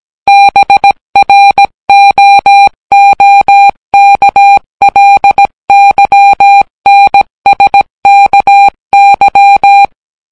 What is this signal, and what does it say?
Morse code, heard as a keyed tone, about 17 wpm, 800 Hz. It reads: BROOKLYNSKY